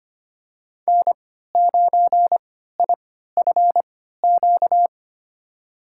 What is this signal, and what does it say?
Morse code, heard as a keyed tone, about 25 wpm, spaced as for 14 wpm, 700 Hz. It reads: N9IFQ
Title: USA Call Signs - 3x Character Spacing - 25wpm